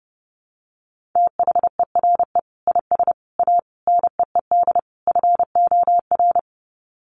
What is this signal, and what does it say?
Morse code, heard as a keyed tone, about 30 wpm, 700 Hz. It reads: THERE IS A NEED FOR
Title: Sets of 5 Words Encore2 30wpm